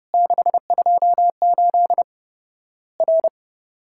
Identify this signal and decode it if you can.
Morse code, heard as a keyed tone, about 30 wpm, 700 Hz. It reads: 628 R
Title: Worked All Europe DX Contest - QTC Exchange - 30wpm